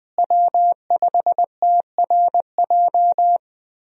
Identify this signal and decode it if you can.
Morse code, heard as a keyed tone, about 20 wpm, 700 Hz. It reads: W5TRJ
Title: USA Call Signs - 5 Characters - Repeated 3x - 20wpm